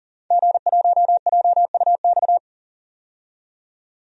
Morse code, heard as a keyed tone, about 40 wpm, 700 Hz. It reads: G1JUX
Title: DXCC-100 Call Signs - 5 Characters - 40wpm